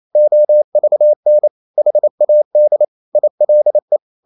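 Morse code, heard as a keyed tone, about 28 wpm, 600 Hz. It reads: OVN HAD ILE
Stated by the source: Danish short words sent in Morse at 28 wpm, from CW med Gnister.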